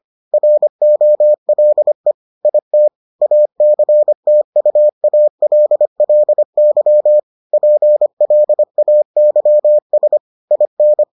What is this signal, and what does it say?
Morse code, heard as a keyed tone, about 25 wpm, 600 Hz. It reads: ROLE IT ACTUALLY PLAYS IN